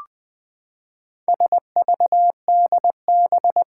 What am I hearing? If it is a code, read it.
Morse code, heard as a keyed tone, about 21 wpm, 700 Hz. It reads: SVDB
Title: Four Challenging Letter-Number Combinations 20wpm